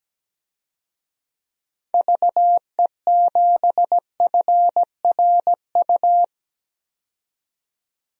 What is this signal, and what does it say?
Morse code, heard as a keyed tone, about 17 wpm, 700 Hz. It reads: VE7FRU
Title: DXCC-100 Call Signs - 6 Characters - 17wpm